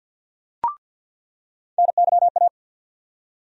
Morse code, heard as a keyed tone, about 47 wpm, 700 Hz. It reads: NYA
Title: USA Call Sign Postfixes - 3 Letters 50wpm